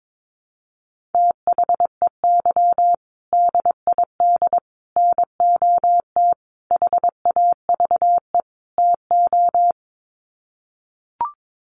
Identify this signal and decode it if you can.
Morse code, heard as a keyed tone, about 22 wpm, 700 Hz. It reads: THEY DID NOT HAVE TO
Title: Sets of 5 Words Encore2 22wpm